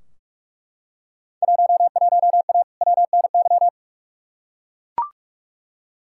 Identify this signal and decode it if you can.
Morse code, heard as a keyed tone, about 41 wpm, 700 Hz. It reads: 11A WNY
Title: Field Day Exchanges Encore - 45wpm